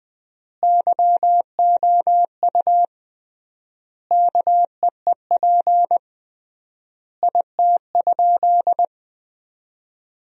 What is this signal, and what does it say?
Morse code, heard as a keyed tone, about 20 wpm, 700 Hz. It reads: YOU KEEP IT?
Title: 3x Word-Spacing - Sentences from Top 300 Words - 20wpm